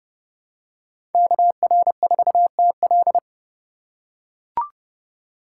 Morse code, heard as a keyed tone, about 28 wpm, 700 Hz. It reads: KR4TL